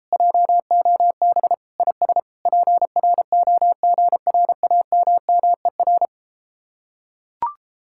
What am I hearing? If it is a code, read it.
Morse code, heard as a keyed tone, about 33 wpm, 700 Hz. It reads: JOB IS PROGRAMMER